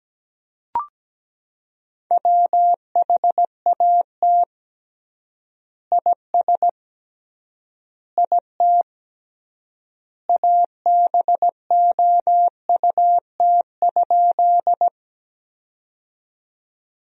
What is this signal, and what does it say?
Morse code, heard as a keyed tone, about 17 wpm, 700 Hz. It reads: WHAT IS IT ABOUT?
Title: 3x Word-Spacing - Sentences from Top 100 Words - 17wpm